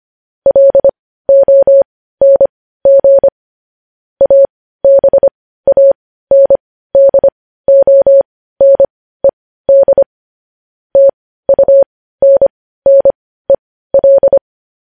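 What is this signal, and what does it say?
Morse code, heard as a keyed tone, about 25 wpm, spaced as for 15 wpm, 550 Hz. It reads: LONG ABANDONED TUNNEL
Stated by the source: Morse news headlines practice at 15.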